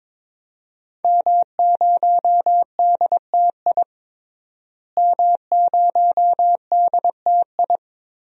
Morse code, heard as a keyed tone, about 22 wpm, 700 Hz. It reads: M0DTI M0DTI
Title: DXCC-100 Call Signs - 5 Characters - Repeated 3x - 22wpm